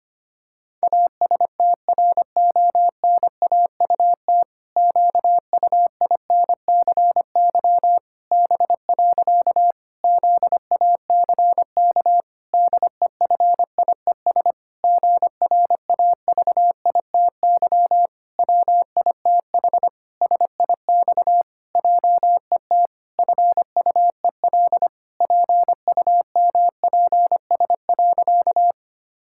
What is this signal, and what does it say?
Morse code, heard as a keyed tone, about 25 wpm, 700 Hz. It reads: ASTRONAUT QUINCY B. ZACK DEFIES GRAVITY WITH SIX JET FUEL PUMPS.
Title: Pangram Challenge v2 - 25wpm